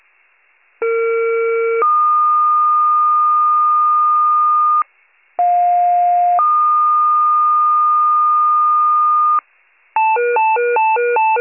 Two-tone page, heard as a one-second tone, then a three-second tone, about 460 then 1220 Hz, about 710 then 1220 Hz. It is heard from 0.8 to 9.4 s.